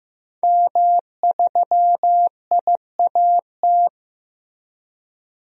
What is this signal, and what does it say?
Morse code, heard as a keyed tone, about 15 wpm, 700 Hz. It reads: M3IAT
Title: DXCC-100 Call Signs - 5 Characters - Repeated 2x - 15wpm